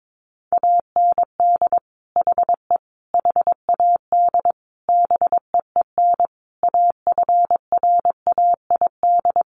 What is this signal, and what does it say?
Morse code, heard as a keyed tone, about 22 wpm, 700 Hz. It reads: AND HE HAD BEEN AFRAID
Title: Sentences from Top 1500 Words 22wpm